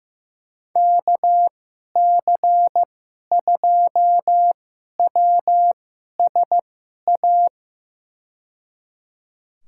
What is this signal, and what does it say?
Morse code, heard as a keyed tone, about 16 wpm, spaced as for 11 wpm, 700 Hz. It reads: KC2WSA
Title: USA Call Signs - 6 Characters - 2x Character Spacing - 15wpm